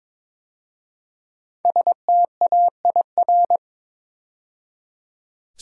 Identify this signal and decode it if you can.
Morse code, heard as a keyed tone, about 22 wpm, 700 Hz. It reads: STAIR
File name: 5 Letter Words - First 3 Letters Given - 22wpm